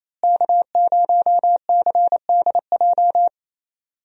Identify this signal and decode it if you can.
Morse code, heard as a keyed tone, about 28 wpm, 700 Hz. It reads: K0CDJ
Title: USA Call Signs - 5 Characters - Repeated 2x - 28wpm